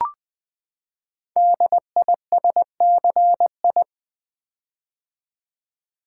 Morse code, heard as a keyed tone, about 20 wpm, 700 Hz. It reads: DISCI
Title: Build A Word - 5 Letters At A Time - 10-Letter Words - 20wpm